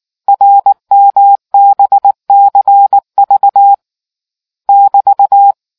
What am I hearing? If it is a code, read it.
Morse code, heard as a keyed tone, about 19 wpm, 800 Hz. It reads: RMBCV <BT>